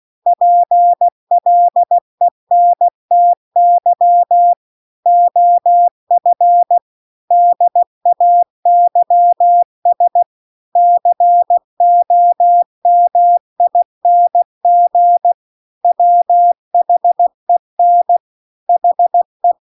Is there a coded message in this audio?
Yes: Morse code from 0.3 to 19.5 s, heard as a keyed tone, about 16 wpm, 700 Hz. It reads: PLENTY OF DAYS COMING WHEN HE